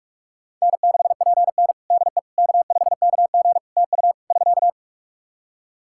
Morse code, heard as a keyed tone, about 46 wpm, 700 Hz. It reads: N6PN DE K5KG TU <SK>
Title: Practice QSO - Exchange 4 - Ending - 45wpm